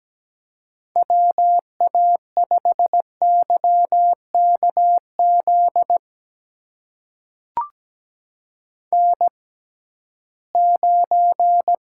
Morse code, heard as a keyed tone, about 17 wpm, spaced as for 13 wpm, 700 Hz. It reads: WA5YKZ N9
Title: USA Call Signs - 6x Character Spacing - 17wpm